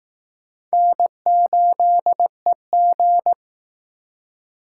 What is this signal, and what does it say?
Morse code, heard as a keyed tone, about 19 wpm, 700 Hz. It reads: N8EG